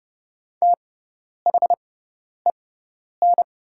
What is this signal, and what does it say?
Morse code, heard as a keyed tone, about 30 wpm, spaced as for 10 wpm, 700 Hz. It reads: THEN
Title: 6x Character Spacing - Top 200 Words - 30wpm